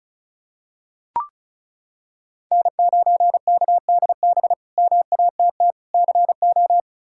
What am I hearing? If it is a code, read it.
Morse code, heard as a keyed tone, about 34 wpm, 700 Hz. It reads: N9KDB MATT CO
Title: K1USN SST QSO Practice - 35wpm